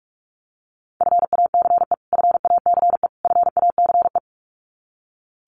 Morse code, heard as a keyed tone, about 44 wpm, 700 Hz. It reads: FACE FACE FACE
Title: Top 500 Words Repeated 45wpm